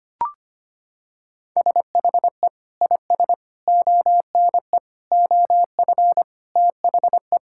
Morse code, heard as a keyed tone, about 25 wpm, 690 Hz. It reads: SHE IS ONE OF THE
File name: practice / 25wpm / Sentences with 2-Word Phrases - 25wpm